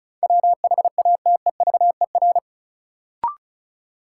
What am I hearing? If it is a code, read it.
Morse code, heard as a keyed tone, about 34 wpm, 700 Hz. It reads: WHATEVER
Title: Sentences from Top 1100 Words - New Word Drill - 35wpm